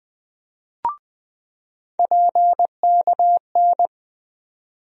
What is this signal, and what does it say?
Morse code, heard as a keyed tone, about 20 wpm, 700 Hz. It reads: PKN
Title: USA Call Sign Postfixes - 3 Letters 20wpm